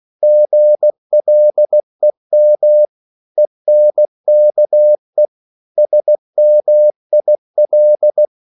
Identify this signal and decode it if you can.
Morse code, heard as a keyed tone, about 16 wpm, 600 Hz. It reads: GLEM ENKE SMIL
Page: Korte danske ord 16wpm